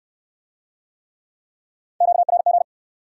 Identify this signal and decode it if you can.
Morse code, heard as a keyed tone, about 67 wpm, 700 Hz. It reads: QRG